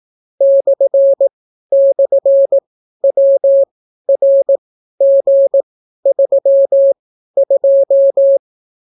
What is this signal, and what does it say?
Morse code, heard as a keyed tone, about 18 wpm, spaced as for 12 wpm, 550 Hz. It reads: //WRG32